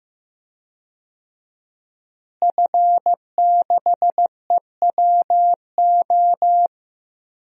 Morse code, heard as a keyed tone, about 15 wpm, 700 Hz. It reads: F6EWO
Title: DXCC-100 Call Signs - 5 Characters - Repeated 3x - 15wpm